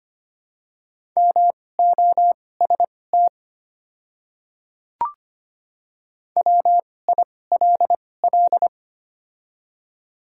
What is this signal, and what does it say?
Morse code, heard as a keyed tone, about 25 wpm, spaced as for 18 wpm, 700 Hz. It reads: MOST WILL
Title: 2x Character Spacing - Top 100 Words - 25wpm